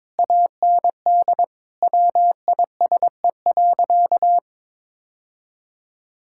Morse code, heard as a keyed tone, about 21 wpm, 700 Hz. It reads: AND WISE.
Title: Sentences from Top 1600 Words 22wpm